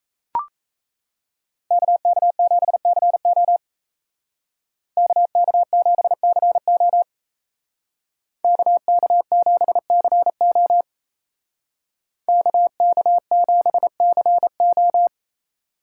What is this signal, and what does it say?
Morse code, heard as a keyed tone, about 36 wpm, 700 Hz. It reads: KK7CO KK7CO KK7CO KK7CO